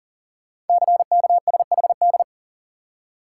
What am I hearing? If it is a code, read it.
Morse code, heard as a keyed tone, about 42 wpm, 700 Hz. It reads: CKSHD
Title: Code Groups - 5 Letters - 40wpm